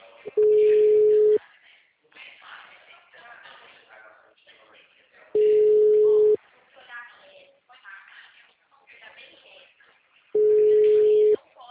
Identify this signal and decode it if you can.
Ringback tone (Europe, 425 Hz), with speech.